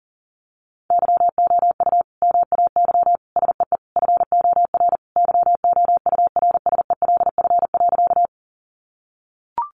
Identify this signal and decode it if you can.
Morse code, heard as a keyed tone, about 40 wpm, 700 Hz. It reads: YOU MAY SEE FOR YOURSELF.
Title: Sentences from Top 600 Words 40wpm